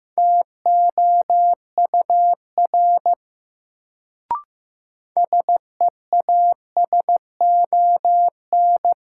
Morse code, heard as a keyed tone, about 15 wpm, 700 Hz. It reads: TOUR SEASON
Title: Sentences from Top 1400 Words - New Word Drill - 15wpm